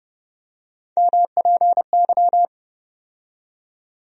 Morse code, heard as a keyed tone, about 30 wpm, 700 Hz. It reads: MPY
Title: USA Call Sign Postfixes - 3 Letters 30wpm